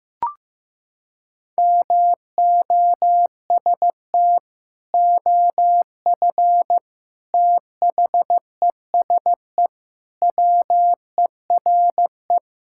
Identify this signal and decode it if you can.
Morse code, heard as a keyed tone, about 15 wpm, 700 Hz. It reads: MOST OF THESE WERE
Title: Sentences from Top 1000 Words - Review Part 5 - 15wpm